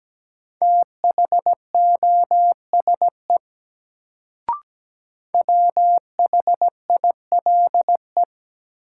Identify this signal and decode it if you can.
Morse code, heard as a keyed tone, about 17 wpm, 700 Hz. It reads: THOSE WHILE
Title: Sentences from Top 300 Words - New Word Drill - 17wpm